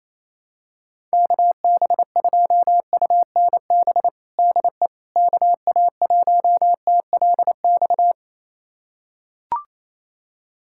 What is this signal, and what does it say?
Morse code, heard as a keyed tone, about 28 wpm, 700 Hz. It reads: KB2UNB DE KA1TLX
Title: USA Callsign DE Callsign - 6 Characters - 28wpm